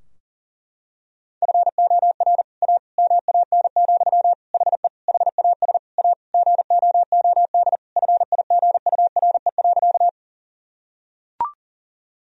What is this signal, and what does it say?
Morse code, heard as a keyed tone, about 40 wpm, 700 Hz. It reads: FOR A MAN, HE HAS A GOOD FIGURE.
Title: Sentences from Top 1000 Words - Review Part 2 - 40wpm